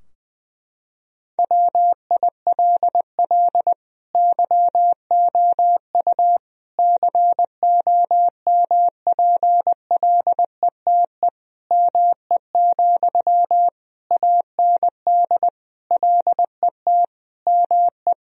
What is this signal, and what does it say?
Morse code, heard as a keyed tone, about 21 wpm, 700 Hz. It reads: WILL YOU COMPLETE ME, AND LET ME
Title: Sentences from Top 1000 Words - Review Part 2 - 20wpm